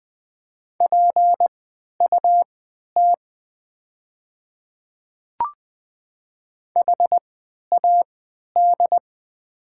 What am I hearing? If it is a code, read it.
Morse code, heard as a keyed tone, about 20 wpm, spaced as for 11 wpm, 690 Hz. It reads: PUT HAD
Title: 3x Character Spacing - Top 500 Words - 20wpm